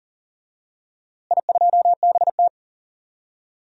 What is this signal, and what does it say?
Morse code, heard as a keyed tone, about 40 wpm, 700 Hz. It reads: I2BT
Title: DXCC-100 Call Signs - 4 Characters - Repeated 3x - 40wpm